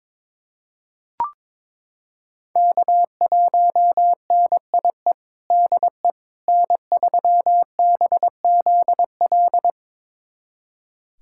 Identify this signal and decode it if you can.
Morse code, heard as a keyed tone, about 21 wpm, 700 Hz. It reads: K1NIE DE N3BZL
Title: USA Callsign DE Callsign - 5 Characters - 22wpm